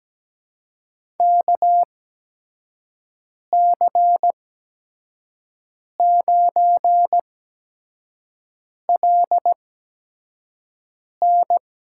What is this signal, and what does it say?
Morse code, heard as a keyed tone, about 17 wpm, spaced as for 5 wpm, 700 Hz. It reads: KC9LN
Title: USA Call Signs - 8x Character Spacing - 17wpm